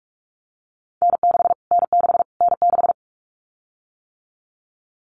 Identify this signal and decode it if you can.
Morse code, heard as a keyed tone, about 44 wpm, 700 Hz. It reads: N6 N6 N6
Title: Callsign Prefixes for Top 25 DXCC - Repeated 3x - 45wpm